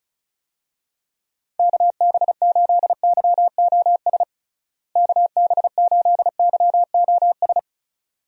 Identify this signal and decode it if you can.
Morse code, heard as a keyed tone, about 34 wpm, 700 Hz. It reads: KB8YOS KB8YOS
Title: USA Call Signs - 6 Characters - Repeated 2x - 35wpm